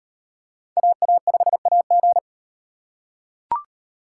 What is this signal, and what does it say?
Morse code, heard as a keyed tone, about 38 wpm, 700 Hz. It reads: AA5AG